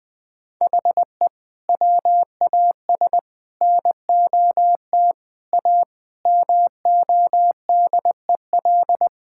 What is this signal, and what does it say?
Morse code, heard as a keyed tone, about 20 wpm, 700 Hz. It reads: HE WAS NOT A MODEL
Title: Sentences from Top 1000 Words - Review Part 5 - 20wpm